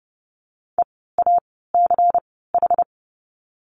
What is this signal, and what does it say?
Morse code, heard as a keyed tone, about 30 wpm, spaced as for 17 wpm, 700 Hz. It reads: EACH
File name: course / 30:10wpm / 230 Lesson 37f - Letter x - Words 30wpm